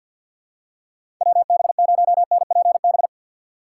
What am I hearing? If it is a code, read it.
Morse code, heard as a keyed tone, about 52 wpm, 700 Hz. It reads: WB0NPB
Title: USA Call Signs - 6 Characters - 6x Character Spacing - 50wpm